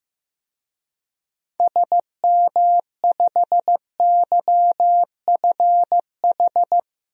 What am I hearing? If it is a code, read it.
Morse code, heard as a keyed tone, about 16 wpm, 700 Hz. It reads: SM5YFH
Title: DXCC-100 Call Signs - 6 Characters - Repeated 2x - 15wpm